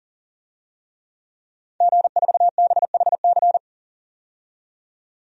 Morse code, heard as a keyed tone, about 40 wpm, 700 Hz. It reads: G4BHC
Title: DXCC-100 Call Signs - 5 Characters - 40wpm